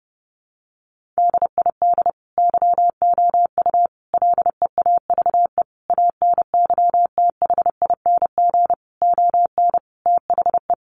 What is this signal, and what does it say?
Morse code, heard as a keyed tone, about 30 wpm, 700 Hz. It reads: DID YOU LEAVE ANYTHING ON THE
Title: Sentences from Top 1000 Words 30wpm